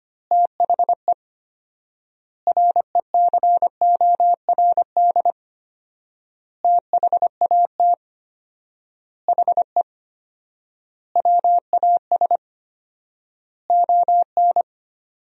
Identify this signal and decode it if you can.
Morse code, heard as a keyed tone, about 25 wpm, 700 Hz. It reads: THE RECORD THAT HE WAS ON
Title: 4x Word-Spacing - Sentences from Top 500 Words - 25wpm